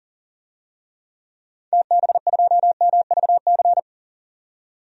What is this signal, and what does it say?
Morse code, heard as a keyed tone, about 40 wpm, 700 Hz. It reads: TB2MVC
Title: DXCC-100 Call Signs - 6 Characters - 40wpm